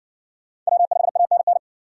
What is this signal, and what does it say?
Morse code, heard as a keyed tone, about 60 wpm, 700 Hz. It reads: W5ANN